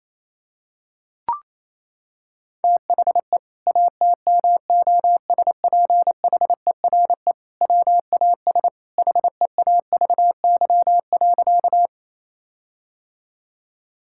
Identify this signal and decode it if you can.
Morse code, heard as a keyed tone, about 28 wpm, 700 Hz. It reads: THE ATMOSPHERE WAS HEAVY.
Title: Sentences from Top 1600 Words 28wpm